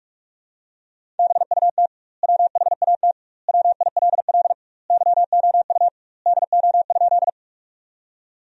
Morse code, heard as a keyed tone, about 45 wpm, 700 Hz. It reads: BUT WHAT WILL YOU DO?
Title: 2x Word-Spacing - Sentences from Top 100 Words - 45wpm